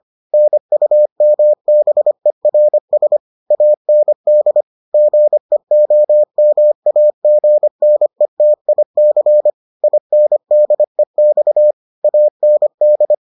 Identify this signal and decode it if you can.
Morse code, heard as a keyed tone, about 25 wpm, 600 Hz. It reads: NUMBERS AND GEOMAGNETIC INDEX AND